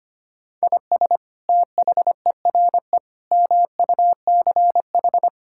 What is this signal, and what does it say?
Morse code, heard as a keyed tone, about 25 wpm, 700 Hz. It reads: IS THERE MUCH
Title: Sentences from Top 1000 Words - Review Part 5 - 25wpm